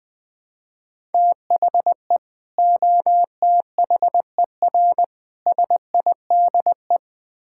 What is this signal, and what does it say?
Morse code, heard as a keyed tone, about 20 wpm, 700 Hz. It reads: THE OTHER SIDE
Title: Build A Phrase - 5-Word Phrases - 20wpm